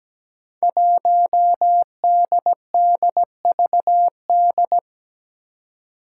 Morse code, heard as a keyed tone, about 17 wpm, 700 Hz. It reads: 1DDVD
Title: Five Challenging Letter-Number Combinations 17wpm